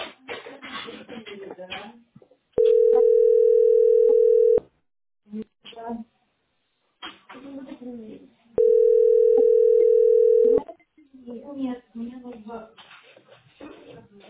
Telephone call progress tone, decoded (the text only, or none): ringback tone